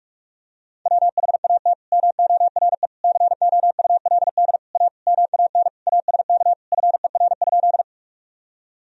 Morse code, heard as a keyed tone, about 43 wpm, 700 Hz. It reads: WHAT MORE COULD A MAN ASK FER?
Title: Sentences from Top 200 Words and Abbreviations 45wpm